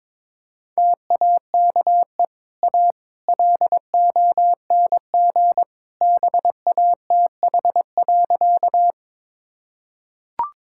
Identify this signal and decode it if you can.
Morse code, heard as a keyed tone, about 21 wpm, 690 Hz. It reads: TAKE A LONG BATH.
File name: practice / 22wpm / Easy Sentences with Hints - 22wpm